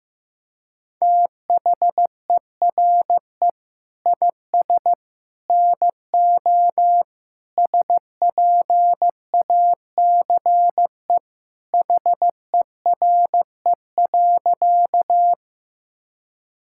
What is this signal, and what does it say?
Morse code, heard as a keyed tone, about 15 wpm, 700 Hz. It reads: THERE IS NO SPACE HERE.
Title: Sentences from Top 1000 Words - Review Part 5 - 15wpm